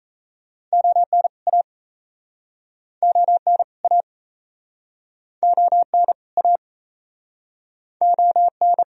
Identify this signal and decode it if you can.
Morse code, heard as a keyed tone, about 40 wpm, 700 Hz. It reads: ON A ON A ON A ON